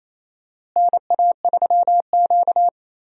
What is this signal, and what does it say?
Morse code, heard as a keyed tone, about 27 wpm, 700 Hz. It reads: NA3Q
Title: USA Call Signs - 4 Characters - Repeated 2x - 28wpm